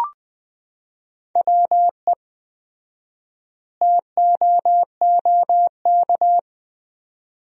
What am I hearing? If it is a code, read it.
Morse code, heard as a keyed tone, about 20 wpm, 700 Hz. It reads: WE TOOK
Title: 4x Word-Spacing - Sentences from Top 400 Words - 20wpm